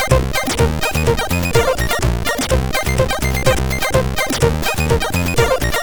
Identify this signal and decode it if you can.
There is no code.